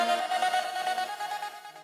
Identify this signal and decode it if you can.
There is no code.